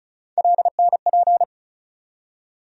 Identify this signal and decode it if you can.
Morse code, heard as a keyed tone, about 36 wpm, 700 Hz. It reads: LNP
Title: USA Call Sign Postfixes - 3 Letters 35wpm